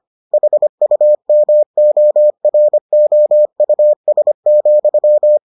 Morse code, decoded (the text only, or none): HUMOROUS,